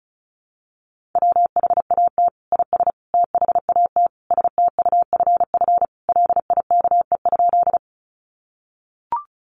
Morse code, heard as a keyed tone, about 34 wpm, 700 Hz. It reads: WHAT IS THAT STUFF LIKE?
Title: Sentences from Top 1400 Words 35wpm